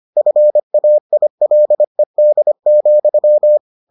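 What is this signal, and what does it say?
Morse code, heard as a keyed tone, about 25 wpm, 600 Hz. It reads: FAILED,
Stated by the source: Morse edition of the Foundations of Amateur Radio episodes.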